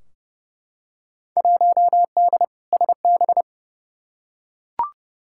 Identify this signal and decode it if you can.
Morse code, heard as a keyed tone, about 30 wpm, 700 Hz. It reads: 1D SB
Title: Field Day Exchanges - 30wpm